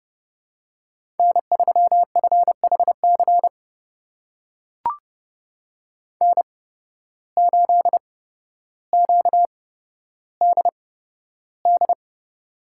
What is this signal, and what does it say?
Morse code, heard as a keyed tone, about 30 wpm, spaced as for 13 wpm, 700 Hz. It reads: N3FHC N8QDD
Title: USA Call Signs - 8x Character Spacing - 30wpm